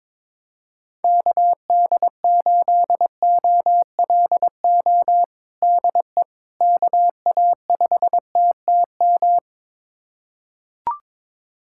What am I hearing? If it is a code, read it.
Morse code, heard as a keyed tone, about 22 wpm, 700 Hz. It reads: KD8OLO DE KA5TTM
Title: USA Callsign DE Callsign 22wpm